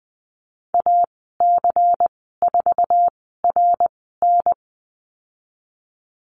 Morse code, heard as a keyed tone, about 20 wpm, spaced as for 15 wpm, 700 Hz. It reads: AC4RN
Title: 122 Lesson 20g - Number 3 - Call Signs 20wpm